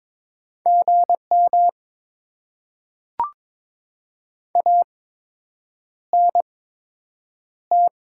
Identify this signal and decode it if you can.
Morse code, heard as a keyed tone, about 22 wpm, spaced as for 8 wpm, 700 Hz. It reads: GM ANT
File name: Top 90 QSO Elements - 8x Character Spacing - 22wpm